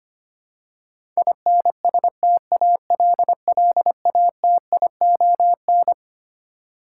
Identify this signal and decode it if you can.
Morse code, heard as a keyed tone, about 25 wpm, 700 Hz. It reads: INSTALLATION
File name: Twelve Letter Words - 25wpm